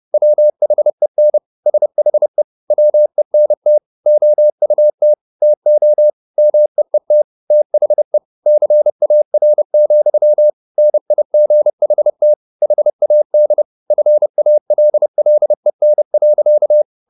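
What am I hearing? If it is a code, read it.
Morse code, heard as a keyed tone, about 30 wpm, 600 Hz. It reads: WHEN SHE WENT OUT TO MEET THE CAR, NIGHT HAD FALLEN.
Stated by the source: Morse code at 30 wpm, the novel Never by Ken Follett in English, from CW med Gnister.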